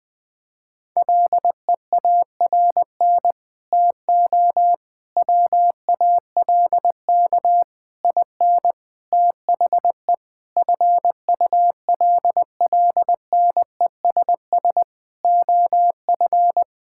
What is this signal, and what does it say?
Morse code, heard as a keyed tone, about 20 wpm, 700 Hz. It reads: LEARN TO WALK IN THE FULLNESS OF
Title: Sentences with 9-Word Phrases - Spoken Beginning - 20wpm